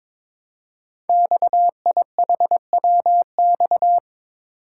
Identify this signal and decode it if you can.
Morse code, heard as a keyed tone, about 22 wpm, 700 Hz. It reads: XIHWX
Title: Code Groups - 5 Letters - 22wpm